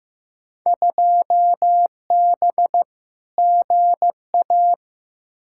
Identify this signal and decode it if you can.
Morse code, heard as a keyed tone, about 15 wpm, 700 Hz. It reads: 2B GA
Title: Field Day Exchanges Encore - 15wpm